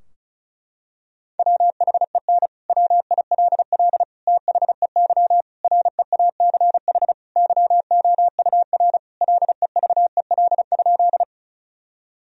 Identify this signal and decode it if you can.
Morse code, heard as a keyed tone, about 36 wpm, 700 Hz. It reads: WHEN WILL THEY REACH YOUR LEVEL?
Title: Sentences from Top 1000 Words - Review Part 4 - 35wpm